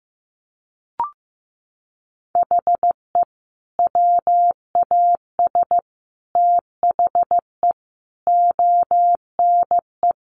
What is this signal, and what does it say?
Morse code, heard as a keyed tone, about 14 wpm, 700 Hz. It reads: HE WAS THE ONE
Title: Sets of 5 Words Encore2 15wpm